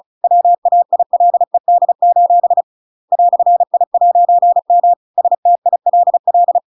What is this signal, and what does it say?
Morse code, heard as a keyed tone, about 34 wpm, 700 Hz. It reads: WAILED: "I'M STILL
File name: Pinocchio / Pinocchio ch7 35 35 ik1hge